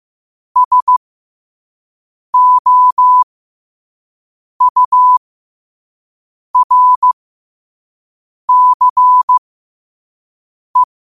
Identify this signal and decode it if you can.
Morse code, heard as a keyed tone, about 15 wpm, spaced as for 5 wpm, 1000 Hz. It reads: SOURCE
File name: qotdfiles / 5wpm